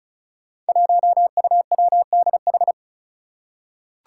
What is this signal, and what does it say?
Morse code, heard as a keyed tone, about 36 wpm, 700 Hz. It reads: 1UWDH
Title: Five Challenging Letter-Number Combinations 35wpm